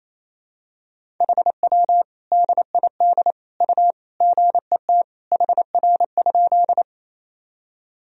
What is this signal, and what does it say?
Morse code, heard as a keyed tone, about 28 wpm, 700 Hz. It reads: HW DID U GET HR?
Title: Sentences from Top 200 Words and Abbreviations 28wpm